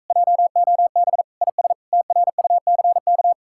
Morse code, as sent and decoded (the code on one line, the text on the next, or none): .--- --- -... / .. ... / - .-. ..- -.-. -.-
JOB IS TRUCK